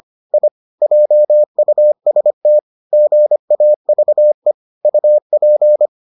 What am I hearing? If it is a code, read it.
Morse code, heard as a keyed tone, about 25 wpm, 600 Hz. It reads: I JUST GAVE UP